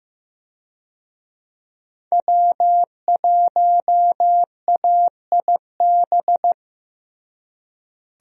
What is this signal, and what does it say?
Morse code, heard as a keyed tone, about 15 wpm, 700 Hz. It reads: W1AIB
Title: USA Call Signs - 5 Characters - Repeated 2x - 15wpm